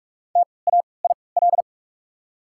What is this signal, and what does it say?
Morse code, heard as a keyed tone, about 44 wpm, spaced as for 25 wpm, 690 Hz. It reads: TAIL
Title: 3x Character Spacing - Top 500 Words - 45wpm